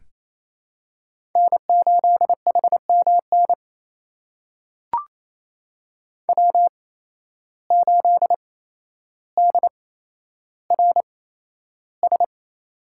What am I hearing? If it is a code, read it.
Morse code, heard as a keyed tone, about 28 wpm, spaced as for 12 wpm, 700 Hz. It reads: N8HMN W8DRS
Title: USA Call Signs - 5 Characters - 8x Character Spacing - 28wpm